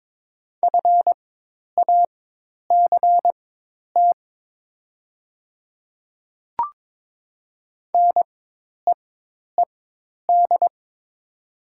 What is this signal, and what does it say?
Morse code, heard as a keyed tone, about 22 wpm, spaced as for 10 wpm, 700 Hz. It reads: FACT NEED
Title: 4x Character Spacing - Top 500 Words - 22wpm